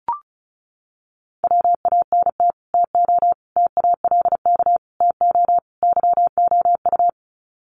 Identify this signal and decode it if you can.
Morse code, heard as a keyed tone, about 34 wpm, 700 Hz. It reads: WANT TO TALK TO YOU
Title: Sets of 5 Words Encore 35wpm